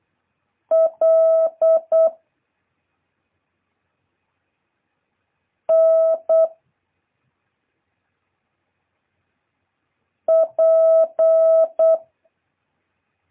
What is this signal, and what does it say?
Morse code, heard as a keyed tone, about 8 wpm, 650 Hz. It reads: L N P
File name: jh8cbh / 2023cw